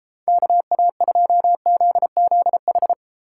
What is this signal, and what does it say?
Morse code, heard as a keyed tone, about 33 wpm, 700 Hz. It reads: KA2ZZH